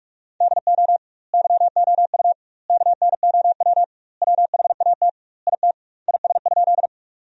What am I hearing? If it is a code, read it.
Morse code, heard as a keyed tone, about 46 wpm, 700 Hz. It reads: DO YOU KNOW WHAT IT IS?
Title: 2x Word-Spacing - Sentences from Top 100 Words - 45wpm